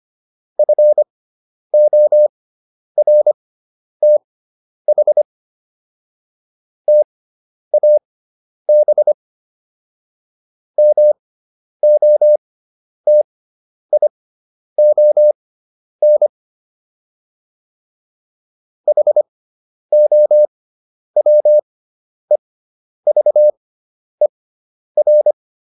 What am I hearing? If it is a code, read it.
Morse code, heard as a keyed tone, about 25 wpm, spaced as for 10 wpm, 600 Hz. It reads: FORTH TAB MOTION HOWEVER